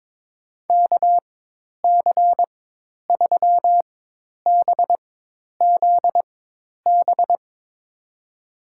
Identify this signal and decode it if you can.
Morse code, heard as a keyed tone, about 23 wpm, spaced as for 10 wpm, 700 Hz. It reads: KC3BZB
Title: USA Call Signs - 6 Characters - 4x Character Spacing - 22wpm